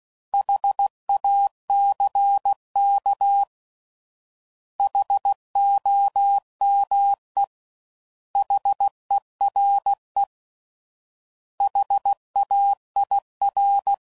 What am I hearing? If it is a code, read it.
Morse code, heard as a keyed tone, about 16 wpm, 800 Hz. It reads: HACK HOME HERE HAIR